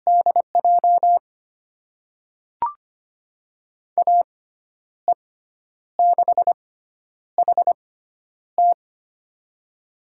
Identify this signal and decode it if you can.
Morse code, heard as a keyed tone, about 25 wpm, spaced as for 10 wpm, 700 Hz. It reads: DJ AE6HT
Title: USA Call Signs - 5 Characters - 6x Character Spacing - 25wpm